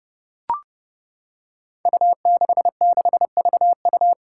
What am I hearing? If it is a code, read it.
Morse code, heard as a keyed tone, about 30 wpm, 700 Hz. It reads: U66VU
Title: Five Challenging Letter-Number Combinations 30wpm